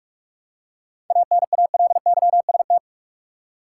Morse code, heard as a keyed tone, about 46 wpm, 700 Hz. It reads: ANALYST
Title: Seven Letter Words - 45wpm